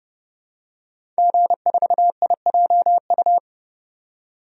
Morse code, heard as a keyed tone, about 30 wpm, 700 Hz. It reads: G4IJU
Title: DXCC-100 Call Signs - 5 Characters - 30wpm